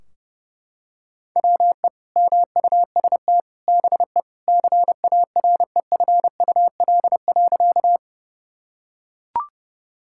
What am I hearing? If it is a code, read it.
Morse code, heard as a keyed tone, about 32 wpm, 700 Hz. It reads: WE MUST BE CAREFUL.
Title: Sentences from Top 1600 Words 30wpm